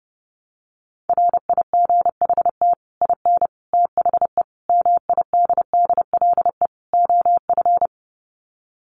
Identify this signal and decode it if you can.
Morse code, heard as a keyed tone, about 28 wpm, 700 Hz. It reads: RIGHT IN THE MIDDLE OF
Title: Sets of 5 Words 30wpm